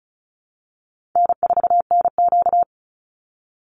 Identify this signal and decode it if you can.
Morse code, heard as a keyed tone, about 34 wpm, 700 Hz. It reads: N4NQ
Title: Easy Call Signs Repeated 3x - 35wpm